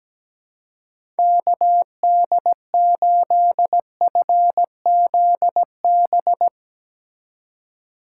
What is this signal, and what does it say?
Morse code, heard as a keyed tone, about 17 wpm, 700 Hz. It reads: KD8FZB
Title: USA Call Signs - 6 Characters - 3x Character Spacing - 17wpm